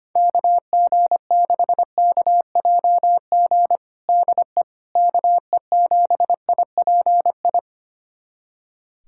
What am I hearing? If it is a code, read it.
Morse code, heard as a keyed tone, about 25 wpm, 700 Hz. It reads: KG6KJG DE KE7IPI
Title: USA Callsign DE Callsign - 6 Characters - 25wpm